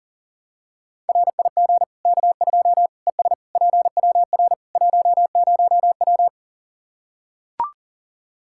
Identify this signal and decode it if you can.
Morse code, heard as a keyed tone, about 40 wpm, 700 Hz. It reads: RIG K2 ES PWR 10W
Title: Practice QSO - Exchange 2 - Info - 40wpm